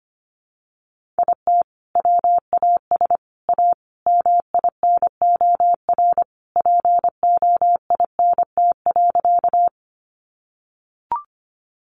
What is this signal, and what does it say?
Morse code, heard as a keyed tone, about 25 wpm, 700 Hz. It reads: IT WAS A MINOR POINT.
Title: Sentences from Top 1500 Words 25wpm